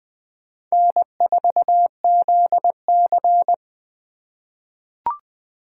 Morse code, heard as a keyed tone, about 20 wpm, 700 Hz. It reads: N4ZC
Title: USA Call Signs - 4 Characters - 3x Character Spacing - 20wpm